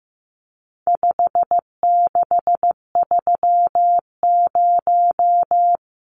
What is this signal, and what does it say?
Morse code, heard as a keyed tone, about 15 wpm, 700 Hz. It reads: 5630
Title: Numbers - 4-Digit 15wpm